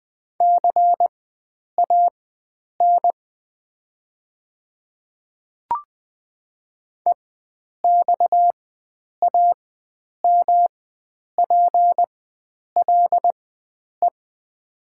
Morse code, heard as a keyed tone, about 20 wpm, spaced as for 9 wpm, 700 Hz. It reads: CAN EXAMPLE
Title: 4x Character Spacing - Top 500 Words - 20wpm